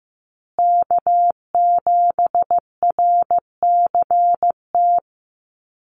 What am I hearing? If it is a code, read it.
Morse code, heard as a keyed tone, about 15 wpm, 700 Hz. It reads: K7RCT
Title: USA Call Signs Repeated 3x 15wpm